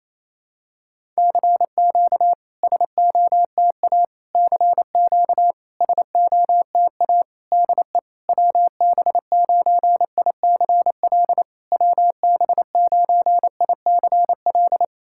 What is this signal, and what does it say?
Morse code, heard as a keyed tone, about 27 wpm, 690 Hz. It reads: CQ SOTA CQ SOTA DE WB9ICL WB9ICL
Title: SOTA QSO Practice 28wpm